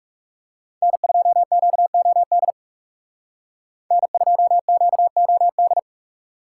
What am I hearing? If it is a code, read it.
Morse code, heard as a keyed tone, about 36 wpm, 700 Hz. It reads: N2QOD N2QOD
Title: Speed Racing - USA 1x3 Call Signs 30wpm